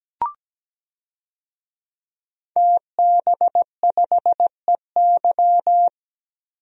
Morse code, heard as a keyed tone, about 17 wpm, 700 Hz. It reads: TB5EY
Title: DXCC-100 Call Signs - 5 Characters - Repeated 3x - 17wpm